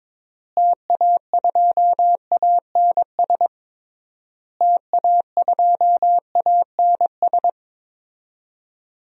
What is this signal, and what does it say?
Morse code, heard as a keyed tone, about 22 wpm, 700 Hz. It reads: TA2ANS TA2ANS
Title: DXCC-100 Call Signs - 6 Characters - Repeated 3x - 22wpm